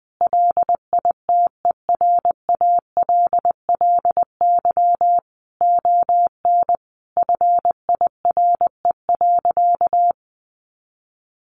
Morse code, heard as a keyed tone, about 20 wpm, 700 Hz. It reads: LITERALLY ON FIRE.
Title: Sentences from Top 1400 Words 20wpm